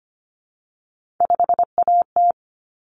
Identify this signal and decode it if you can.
Morse code, heard as a keyed tone, about 25 wpm, 700 Hz. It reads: 5AT
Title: Common Cut Numbers - 3-Digit 25wpm